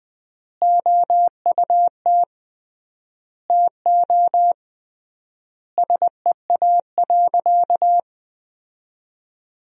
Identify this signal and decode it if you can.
Morse code, heard as a keyed tone, about 20 wpm, 700 Hz. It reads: OUT TO SEA.
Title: 3x Word-Spacing - Sentences from Top 400 Words - 20wpm